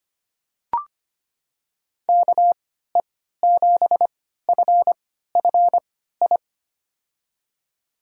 Morse code, heard as a keyed tone, about 25 wpm, spaced as for 14 wpm, 700 Hz. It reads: KE7FFI
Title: USA Call Signs - 6 Characters - 3x Character Spacing - 25wpm